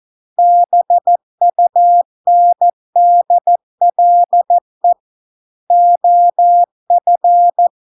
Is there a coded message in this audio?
Yes: Morse code from 0.4 to 7.7 s, heard as a keyed tone, about 14 wpm, 700 Hz. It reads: BUNDLE OF